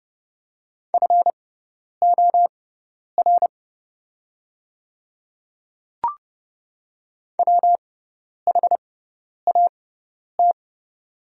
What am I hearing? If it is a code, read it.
Morse code, heard as a keyed tone, about 30 wpm, spaced as for 10 wpm, 700 Hz. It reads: FOR WHAT
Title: 6x Character Spacing - Top 200 Words - 30wpm